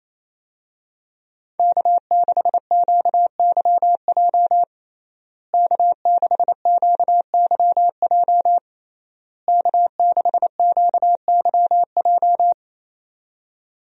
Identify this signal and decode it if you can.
Morse code, heard as a keyed tone, about 28 wpm, 700 Hz. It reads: K6QYJ K6QYJ K6QYJ
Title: USA Call Signs - 5 Characters - Repeated 3x - 28wpm